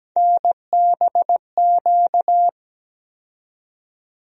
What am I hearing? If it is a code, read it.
Morse code, heard as a keyed tone, about 17 wpm, 700 Hz. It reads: NBQ
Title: USA Call Sign Postfixes - 3 Letters 17wpm